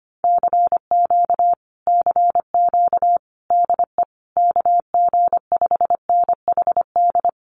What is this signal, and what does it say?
Morse code, heard as a keyed tone, about 25 wpm, 700 Hz. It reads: CQ CQ DE KG5NHD
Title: Common QSO Phrases Encore 25wpm